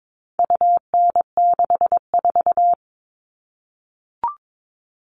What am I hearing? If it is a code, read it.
Morse code, heard as a keyed tone, about 22 wpm, 700 Hz. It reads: UN64
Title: All Cut Numbers - 4-Digit 22wpm